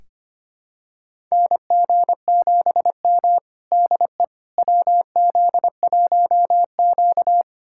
Morse code, heard as a keyed tone, about 25 wpm, 700 Hz. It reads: NG7M DE WZ1Q
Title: USA Callsign DE Callsign - 4 Characters - 25wpm